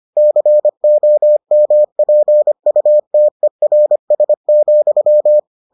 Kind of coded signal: Morse code

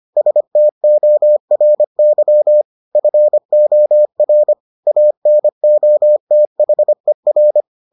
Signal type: Morse code

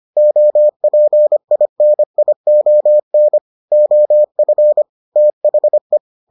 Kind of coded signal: Morse code

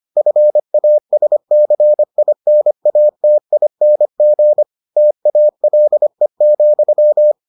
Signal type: Morse code

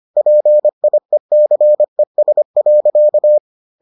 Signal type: Morse code

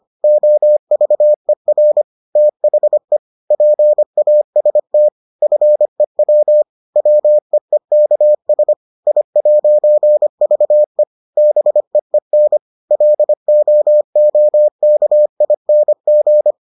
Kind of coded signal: Morse code

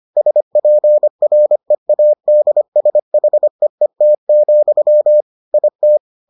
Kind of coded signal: Morse code

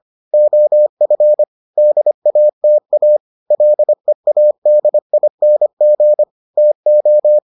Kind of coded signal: Morse code

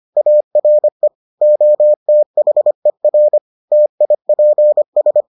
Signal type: Morse code

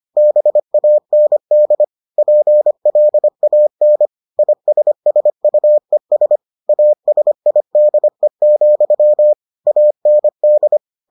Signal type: Morse code

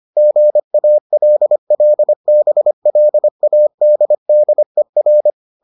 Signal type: Morse code

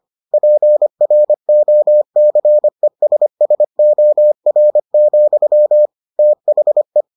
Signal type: Morse code

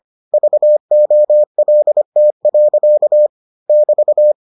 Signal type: Morse code

